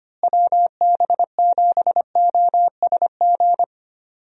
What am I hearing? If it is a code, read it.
Morse code, heard as a keyed tone, about 25 wpm, 700 Hz. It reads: WB7OSG
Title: USA Call Signs - 6 Characters - Repeated 3x - 25wpm